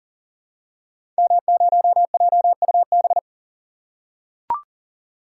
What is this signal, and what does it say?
Morse code, heard as a keyed tone, about 40 wpm, 700 Hz. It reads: M0JUB
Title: Code Groups - 5 Letter:Numbers - 40wpm